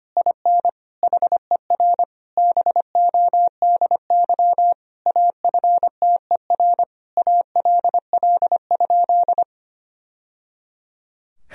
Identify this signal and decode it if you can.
Morse code, heard as a keyed tone, about 25 wpm, 700 Hz. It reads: IN HER BODY AFTER ALL?
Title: Sentences from Top 1000 Words - Review Part 2 - 25wpm